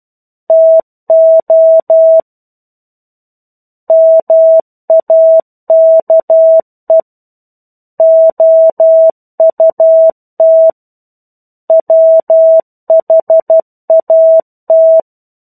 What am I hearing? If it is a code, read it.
Morse code, heard as a keyed tone, about 12 wpm, 650 Hz. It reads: TO MAKE OUT WHAT
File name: 01-Chapter-01 12 WPM